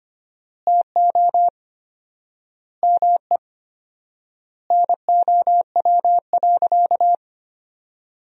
Morse code, heard as a keyed tone, about 25 wpm, 700 Hz. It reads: TO ME NOW.
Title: 4x Word-Spacing - Sentences from Top 500 Words - 25wpm